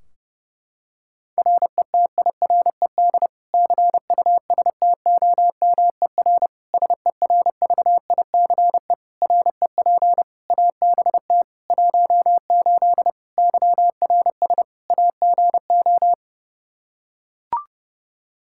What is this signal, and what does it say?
Morse code, heard as a keyed tone, about 30 wpm, 700 Hz. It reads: RETIRED CUSTOMER SERVICE REP ABT 18 YRS AGO
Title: Practice QSO - Exchange 3 - More Info - 30wpm